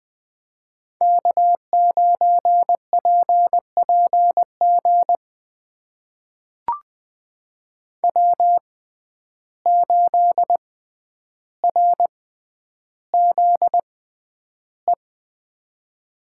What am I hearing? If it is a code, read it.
Morse code, heard as a keyed tone, about 20 wpm, spaced as for 10 wpm, 700 Hz. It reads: K9PPG W8RZE
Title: USA Call Signs - 5 Characters - 6x Character Spacing - 20wpm